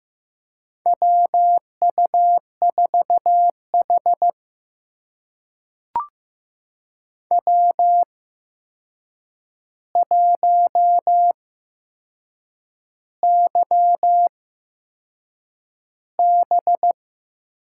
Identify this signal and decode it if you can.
Morse code, heard as a keyed tone, about 15 wpm, spaced as for 6 wpm, 700 Hz. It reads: WU4H W1YB
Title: USA Call Signs - 4 Characters - 8x Character Spacing - 15wpm